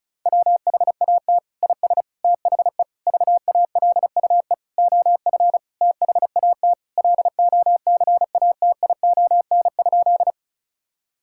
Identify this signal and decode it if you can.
Morse code, heard as a keyed tone, about 36 wpm, 690 Hz. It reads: WHAT IS THE VALUE OF THAT LOCATION?
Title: Sentences from Top 1000 Words - Review Part 6 - 35wpm